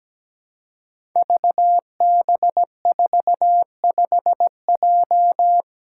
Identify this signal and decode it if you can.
Morse code, heard as a keyed tone, about 17 wpm, 700 Hz. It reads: VB45J
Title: Code Groups - 5 Letter:Numbers - 17wpm